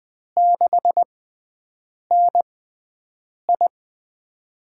Morse code, heard as a keyed tone, about 20 wpm, spaced as for 7 wpm, 700 Hz. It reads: 6NI
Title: USA Call Signs - 4 Characters - 6x Character Spacing - 20wpm